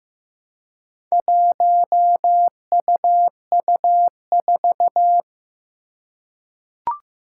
Morse code, heard as a keyed tone, about 15 wpm, 700 Hz. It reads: 1UU4
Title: Four Challenging Letter-Number Combinations 15wpm